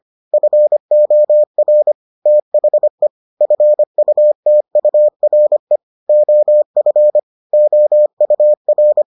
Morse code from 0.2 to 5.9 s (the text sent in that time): FOR THE FUTURE